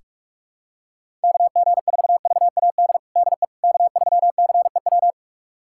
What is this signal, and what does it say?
Morse code, heard as a keyed tone, about 44 wpm, 700 Hz. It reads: KG4VAD DE K3CEW
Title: USA Callsign DE Callsign 45wpm